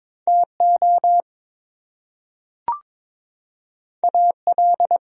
Morse code, heard as a keyed tone, about 22 wpm, 700 Hz. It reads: TO AL
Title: Top 100 Words Repeated 22wpm